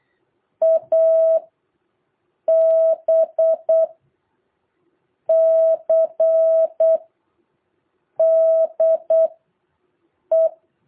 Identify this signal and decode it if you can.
Morse code, heard as a keyed tone, about 8 wpm, spaced as for 5 wpm, 650 Hz. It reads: ABCDE
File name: jh8cbh / 2023cw